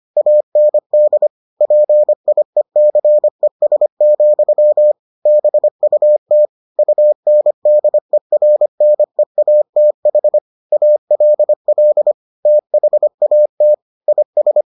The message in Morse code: AND PIECES, BUT UNDERNEATH ALL THAT IS

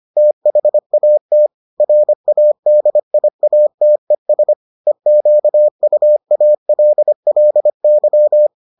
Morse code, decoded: THAT RADIATES EQUALLY

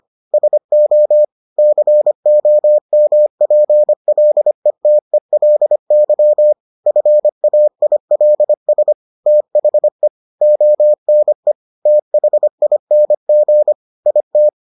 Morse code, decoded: SO COMPLETELY FAILS THE ONE THING IT

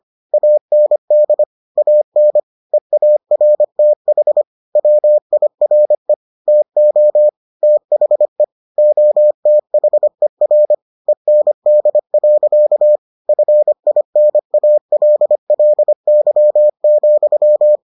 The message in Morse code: AND AN EARTH WIRE TO THE OTHER END. FINALLY,

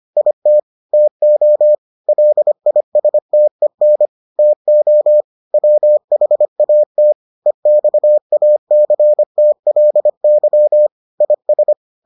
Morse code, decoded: IT TO LISTEN TO WHAT EXACTLY IS

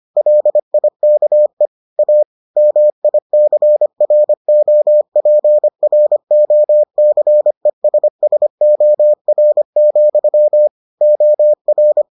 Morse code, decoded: LIKE A MICROPROCESSOR, OR

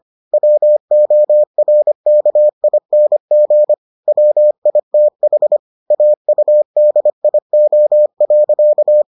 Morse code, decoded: WORKING WITH AUDIO.

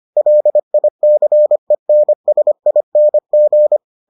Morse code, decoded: LICENSING

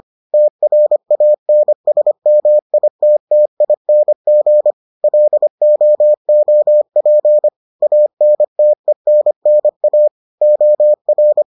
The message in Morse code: TRANSMITTING LOOP ANTENNA OR

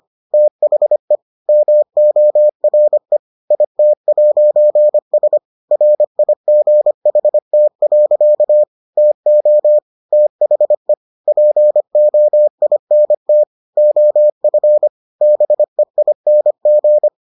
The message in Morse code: THE MORE IT'S RIGHT. TO THE POINT OF BEING

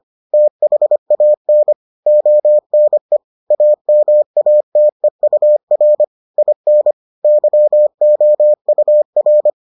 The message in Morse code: THAN ONE AMATEUR IN YOUR

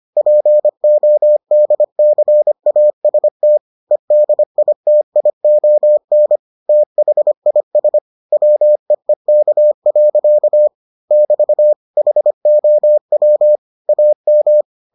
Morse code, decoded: PODCAST EDITION THIS WEEK. <BT> HOW AM

